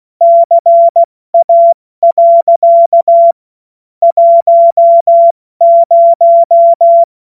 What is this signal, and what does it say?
Morse code, heard as a keyed tone, about 16 wpm, 680 Hz. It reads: CA. 10
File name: Morsekurs / Witze / Witz 22. August 2023 70 F80